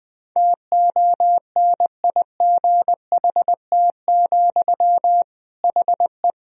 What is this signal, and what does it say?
Morse code, heard as a keyed tone, about 20 wpm, 700 Hz. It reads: TONIGHT, HE
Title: Sentences from Top 1000 Words - Review Part 3 - 20wpm